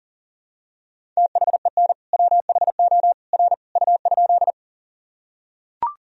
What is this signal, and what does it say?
Morse code, heard as a keyed tone, about 40 wpm, 700 Hz. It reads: THEN WHO R U?
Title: Sentences from Top 100 Words and Abbreviations 40wpm